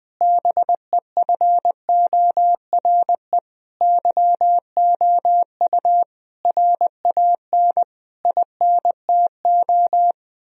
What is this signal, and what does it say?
Morse code, heard as a keyed tone, about 20 wpm, 700 Hz. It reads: BEFORE YOU RAN INTO
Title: Sentences from Top 1000 Words - Review Part 3 - 20wpm